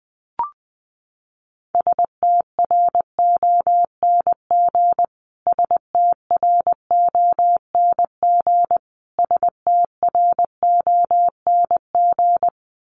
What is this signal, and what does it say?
Morse code, heard as a keyed tone, about 20 wpm, 700 Hz. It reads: STRONG STRONG STRONG
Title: Top 500 Words Repeated 20wpm